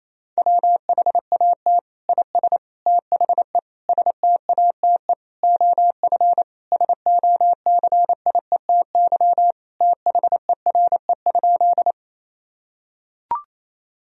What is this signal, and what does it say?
Morse code, heard as a keyed tone, about 28 wpm, 700 Hz. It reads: WHAT IS THE STATE OF SOCIETY THERE?
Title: Sentences from Top 1000 Words - Review Part 4 - 28wpm